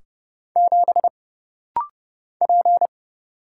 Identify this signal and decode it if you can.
Morse code, heard as a keyed tone, about 30 wpm, 700 Hz. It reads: 7 P
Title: Single Letter-Number - Rapid-Fire 30wpm